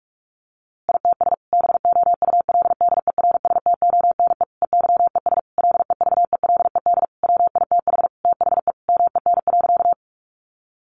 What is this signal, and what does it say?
Morse code, heard as a keyed tone, about 44 wpm, 700 Hz. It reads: ITS BOULDERSTONE EYES LEVELED WITH THE MEN.
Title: A Sound Of Thunder 45wpm